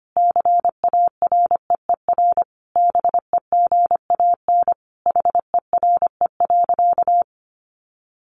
Morse code, heard as a keyed tone, about 25 wpm, 700 Hz. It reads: CAREER BEGAN HERE.
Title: Sentences from Top 1100 Words 25wpm